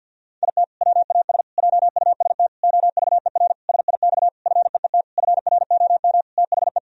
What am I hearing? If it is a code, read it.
Morse code, heard as a keyed tone, about 51 wpm, 700 Hz. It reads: IT WAS JUST OVER SIX FEET FROM THE
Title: Sentences from Top 1000 Words - Review Part 2 - 50wpm